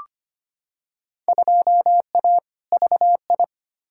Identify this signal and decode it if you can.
Morse code, heard as a keyed tone, about 25 wpm, 700 Hz. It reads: 2A VI
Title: Field Day Exchanges Encore - 25wpm